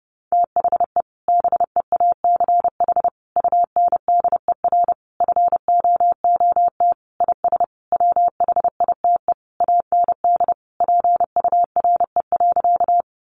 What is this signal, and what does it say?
Morse code, heard as a keyed tone, about 28 wpm, 700 Hz. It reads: THE BEACH UNDER FOOT IS WHITE AND PURE.
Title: Sentences from Top 1500 Words 30wpm